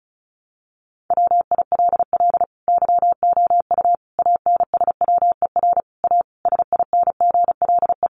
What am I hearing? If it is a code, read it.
Morse code, heard as a keyed tone, about 34 wpm, 700 Hz. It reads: WILL YOU ANSWER A SINGLE
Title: Sentences from Top 700 Words 35wpm